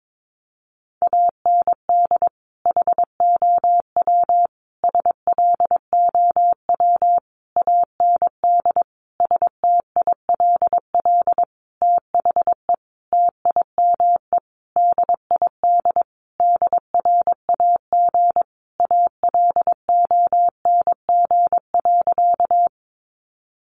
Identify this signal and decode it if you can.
Morse code, heard as a keyed tone, about 22 wpm, 700 Hz. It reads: AND HOW SLOW AND STILL THE TIME DID DRAG ALONG.
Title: Sentences from Top 1500 Words 22wpm